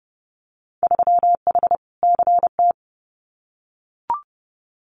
Morse code, heard as a keyed tone, about 30 wpm, 700 Hz. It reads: 3H CT
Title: Winter Field Day Exchanges Encore 30wpm